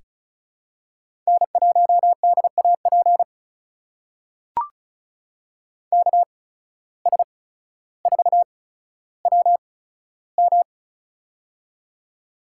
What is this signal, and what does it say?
Morse code, heard as a keyed tone, about 34 wpm, spaced as for 15 wpm, 700 Hz. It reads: N1DAP KS4WM
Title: USA Call Signs - 5 Characters - 8x Character Spacing - 35wpm